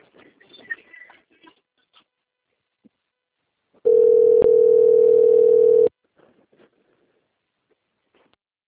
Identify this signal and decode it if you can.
Ringback tone (North America), 440 + 480 Hz.